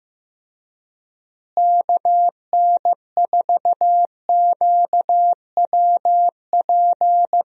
Morse code, heard as a keyed tone, about 15 wpm, 700 Hz. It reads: KN4QWP
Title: USA Call Signs - 6 Characters - Repeated 3x - 15wpm